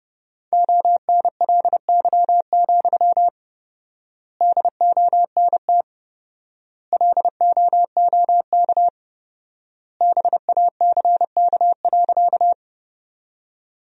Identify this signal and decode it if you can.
Morse code, heard as a keyed tone, about 30 wpm, 700 Hz. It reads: ONLY, DONT LOOK BACK.
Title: 4x Word-Spacing - Sentences from Top 300 Words - 30wpm